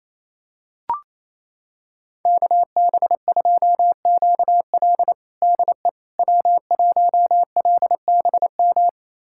Morse code, heard as a keyed tone, about 28 wpm, 700 Hz. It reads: KB2QL DE W1LBM
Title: USA Callsign DE Callsign - 5 Characters - 28wpm